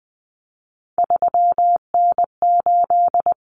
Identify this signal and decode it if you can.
Morse code, heard as a keyed tone, about 20 wpm, 700 Hz. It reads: 3N8
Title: Common Cut Numbers - 3-Digit 20wpm